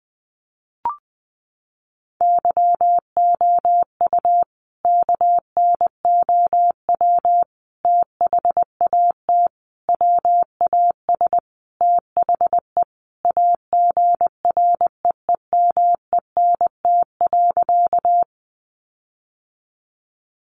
Morse code, heard as a keyed tone, about 20 wpm, 700 Hz. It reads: YOU KNOW THAT WAS THE AGREEMENT.
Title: Sentences from Top 1500 Words 20wpm